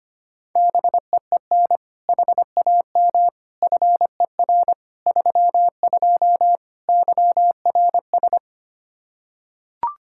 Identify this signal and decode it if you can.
Morse code, heard as a keyed tone, about 25 wpm, 700 Hz. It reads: BEEN HAM FER 32 YRS
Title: K1USN SST QSO Practice 25wpm